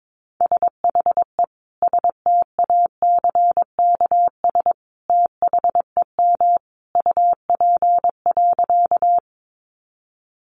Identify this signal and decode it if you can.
Morse code, heard as a keyed tone, about 22 wpm, 700 Hz. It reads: SHE STACKS THEM UP.
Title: Easy Phrases 22wpm